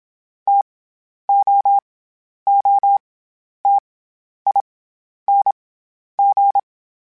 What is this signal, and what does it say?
Morse code, heard as a keyed tone, about 27 wpm, 800 Hz. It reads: T O O T I N G
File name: Variable Spacing / Variable Spacing-mp3s / T AOEs @25